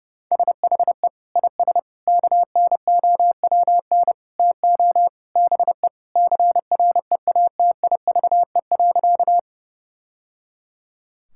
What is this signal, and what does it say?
Morse code, heard as a keyed tone, about 30 wpm, 700 Hz. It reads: SHE IS KNOWN TO BE CREATIVE.
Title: Sentences from Top 1000 Words - Review Part 5 - 30wpm